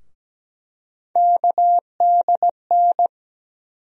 Morse code, heard as a keyed tone, about 17 wpm, 700 Hz. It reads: KDN
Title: USA Call Sign Postfixes - 3 Letters 17wpm